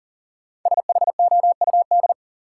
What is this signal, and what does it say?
Morse code, heard as a keyed tone, about 40 wpm, 700 Hz. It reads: SHOUD
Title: Sentences from Top 1000 Words - Review Part 4 - New Word Drill - 40wpm